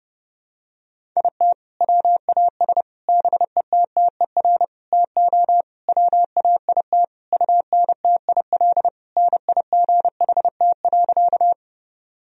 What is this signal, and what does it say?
Morse code, heard as a keyed tone, about 30 wpm, 700 Hz. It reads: IT WAS BETTER TO WAIT UNTIL NIGHT.
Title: Sentences from Top 1000 Words - Review Part 1 - 30wpm